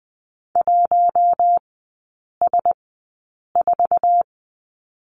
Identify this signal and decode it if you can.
Morse code, heard as a keyed tone, about 20 wpm, spaced as for 8 wpm, 700 Hz. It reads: 1S4
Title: 067 Lesson 12e - Letter d - Three Characters 20wpm